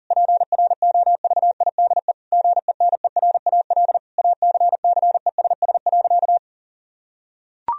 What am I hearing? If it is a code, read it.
Morse code, heard as a keyed tone, about 40 wpm, 700 Hz. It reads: PROVIDE GENERAL ACCESS.
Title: Sentences from Top 1000 Words - Review Part 4 - 40wpm